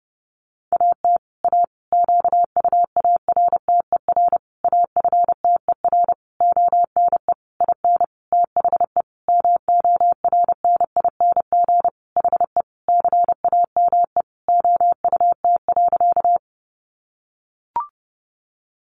Morse code, heard as a keyed tone, about 30 wpm, 700 Hz. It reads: AT A QUARTER AFTER ONE IN THE MORNING HE CAME OUT.
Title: Sentences from Top 1300 Words 30wpm